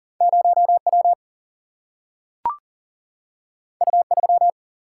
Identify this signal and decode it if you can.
Morse code, heard as a keyed tone, about 40 wpm, 700 Hz. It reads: 0 W U 3
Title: Single-Letter Single-Number 40wpm